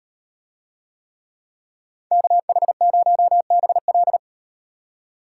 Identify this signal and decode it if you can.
Morse code, heard as a keyed tone, about 38 wpm, 700 Hz. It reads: KH0BL